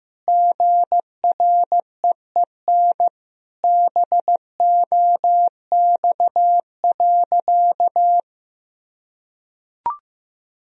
Morse code, heard as a keyed tone, about 15 wpm, 700 Hz. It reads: GREEN BOX.
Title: Sentences from Top 1000 Words - Review Part 2 - 15wpm